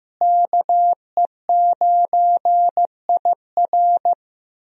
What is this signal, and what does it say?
Morse code, heard as a keyed tone, about 16 wpm, 700 Hz. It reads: KE9IR
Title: USA Call Signs - 5 Characters - Repeated 2x - 15wpm